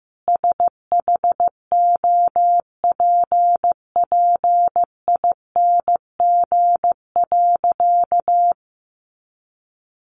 Morse code, heard as a keyed tone, about 15 wpm, 700 Hz. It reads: SHOPPING.
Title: Additional Easy Phrases 15wpm